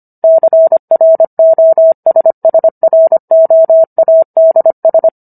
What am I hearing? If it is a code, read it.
Morse code, heard as a keyed tone, about 25 wpm, 650 Hz. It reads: CROSSROADS